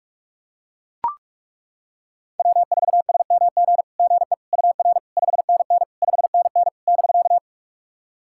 Morse code, heard as a keyed tone, about 41 wpm, 690 Hz. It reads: W4SMG GE UR 5NN 5NN <BK>
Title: SOTA QSO Practice 45wpm